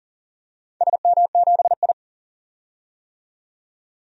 Morse code, heard as a keyed tone, about 37 wpm, 700 Hz. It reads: SM7I
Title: DXCC-100 Call Signs - 4 Characters - 40wpm